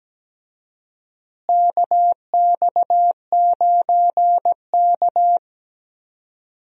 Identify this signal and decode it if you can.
Morse code, heard as a keyed tone, about 17 wpm, 700 Hz. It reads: KX9K
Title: USA Call Signs - 4 Characters - Repeated 2x - 17wpm